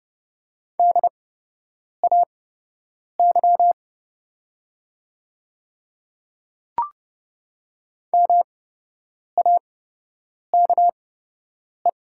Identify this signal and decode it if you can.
Morse code, heard as a keyed tone, about 30 wpm, spaced as for 8 wpm, 700 Hz. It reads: DAY MAKE
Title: 8x Character Spacing - Top 100 Words - 30wpm